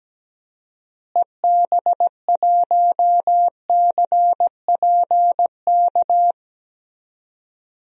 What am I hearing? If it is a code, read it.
Morse code, heard as a keyed tone, about 17 wpm, 700 Hz. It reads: EB1CPK
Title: DXCC-100 Call Signs - 6 Characters - Repeated 3x - 17wpm